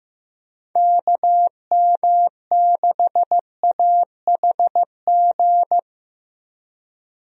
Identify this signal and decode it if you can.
Morse code, heard as a keyed tone, about 15 wpm, 700 Hz. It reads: KM6AHG
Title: USA Call Signs - 6 Characters - Repeated 2x - 15wpm